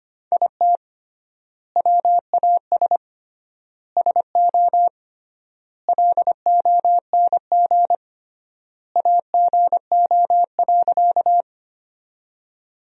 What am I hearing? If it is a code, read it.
Morse code, heard as a keyed tone, about 25 wpm, 700 Hz. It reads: IT WAS SO LONG AGO.
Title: 3x Word-Spacing - Sentences from Top 500 Words - 25wpm